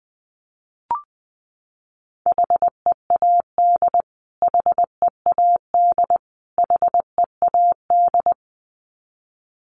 Morse code, heard as a keyed tone, about 19 wpm, 700 Hz. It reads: HEAD HEAD HEAD
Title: Top 500 Words Repeated 20wpm